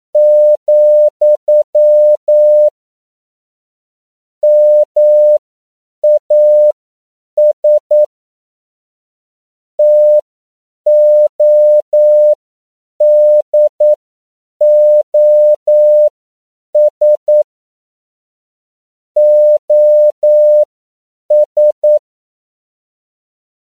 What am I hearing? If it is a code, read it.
Morse code, heard as a keyed tone, about 9 wpm, spaced as for 7 wpm, 600 Hz. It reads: , MAS TODOS OS